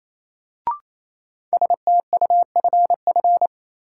Morse code, heard as a keyed tone, about 28 wpm, 700 Hz. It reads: STUFF
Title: Sentences from Top 1400 Words - New Word Drill - 28wpm